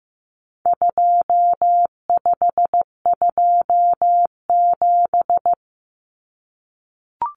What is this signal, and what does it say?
Morse code, heard as a keyed tone, about 15 wpm, 700 Hz. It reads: 2527
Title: Numbers - 4-Digit 15wpm